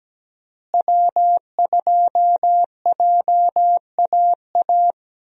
Morse code, heard as a keyed tone, about 17 wpm, 700 Hz. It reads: W2JAA
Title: USA Call Signs - 5 Characters - Repeated 2x - 17wpm